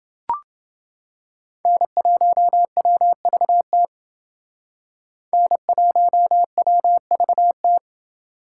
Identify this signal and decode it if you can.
Morse code, heard as a keyed tone, about 28 wpm, 700 Hz. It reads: N1WVT N1WVT